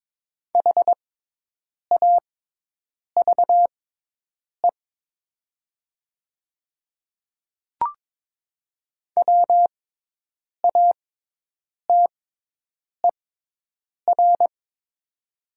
Morse code, heard as a keyed tone, about 22 wpm, spaced as for 8 wpm, 700 Hz. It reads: HAVE WATER
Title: 6x Character Spacing - Top 100 Words - 22wpm